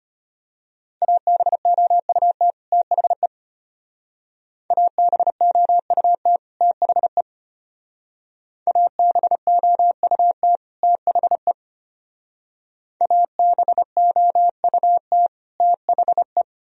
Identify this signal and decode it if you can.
Morse code, heard as a keyed tone, about 30 wpm, 700 Hz. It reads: ABOUT THE ABOUT THE ABOUT THE ABOUT THE
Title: Speed-Racing - Sets of 2 Words 25wpm